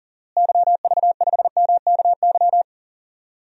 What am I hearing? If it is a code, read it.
Morse code, heard as a keyed tone, about 40 wpm, 700 Hz. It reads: YV5MKY
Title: DXCC-100 Call Signs - 6 Characters - Repeated 3x - 40wpm